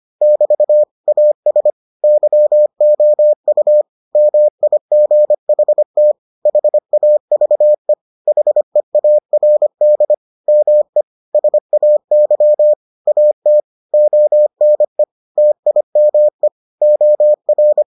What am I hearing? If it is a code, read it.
Morse code, heard as a keyed tone, about 25 wpm, 600 Hz. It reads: <BT> AS YOU MIGHT HAVE HEARD ME SAY AT ONE TIME OR